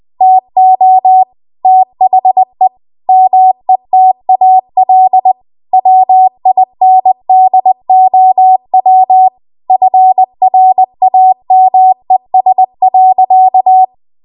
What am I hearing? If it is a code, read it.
Morse code, heard as a keyed tone, about 20 wpm, 750 Hz. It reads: TO THE METAL WINDOW FRAMES.
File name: Archive / 20 WPM / 20WPM